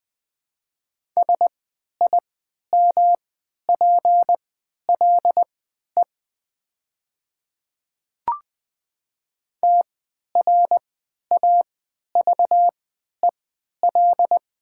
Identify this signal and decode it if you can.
Morse code, heard as a keyed tone, about 20 wpm, spaced as for 11 wpm, 700 Hz. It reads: SIMPLE TRAVEL
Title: 3x Character Spacing - Top 500 Words - 20wpm